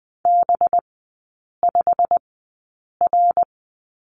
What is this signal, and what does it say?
Morse code, heard as a keyed tone, about 20 wpm, spaced as for 8 wpm, 700 Hz. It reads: B5R
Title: 212 Lesson 34e - Number 8 - Three Characters 20wpm